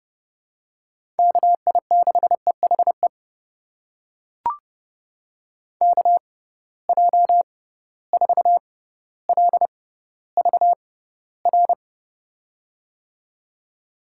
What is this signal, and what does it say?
Morse code, heard as a keyed tone, about 29 wpm, spaced as for 15 wpm, 700 Hz. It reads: KI6EHE KJ4LVR
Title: USA Call Signs - 6 Characters - 6x Character Spacing - 30wpm